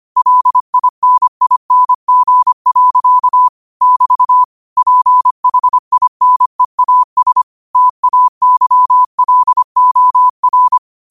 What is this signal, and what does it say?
Morse code, heard as a keyed tone, about 25 wpm, 1000 Hz. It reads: LINING. <BT> PHINEAS TAYLOR
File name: 25wpm